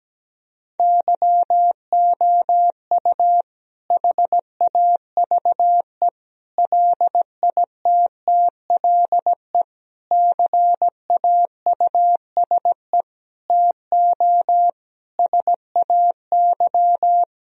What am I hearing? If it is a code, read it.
Morse code, heard as a keyed tone, about 17 wpm, 700 Hz. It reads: YOU HAVE LITTLE CAUSE TO SAY
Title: Sentences from Top 1000 Words - Review Part 1 - 17wpm